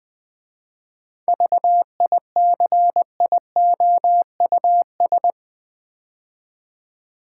Morse code, heard as a keyed tone, about 20 wpm, 700 Hz. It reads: VICIOUS
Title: 7 Letter Words - First 3 Letters Given - 20wpm